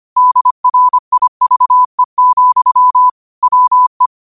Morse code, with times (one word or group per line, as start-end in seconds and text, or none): 0.2-3.1 DRIVE,
3.4-4.1 WE